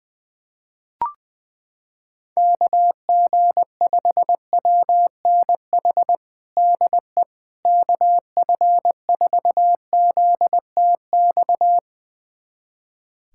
Morse code, heard as a keyed tone, about 21 wpm, 700 Hz. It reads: KG5WNH DE KF4ZTX
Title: USA Callsign DE Callsign 20wpm